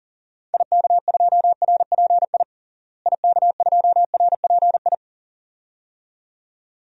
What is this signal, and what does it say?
Morse code, heard as a keyed tone, about 40 wpm, 700 Hz. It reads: IK2RPI IK2RPI
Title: DXCC-100 Call Signs - 6 Characters - Repeated 2x - 40wpm